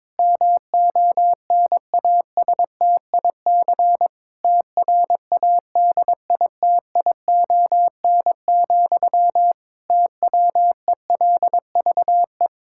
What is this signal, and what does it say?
Morse code, heard as a keyed tone, about 22 wpm, 700 Hz. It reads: MONASTIC TRADITION, TWELVE
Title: Pangram Challenge v2 - 22wpm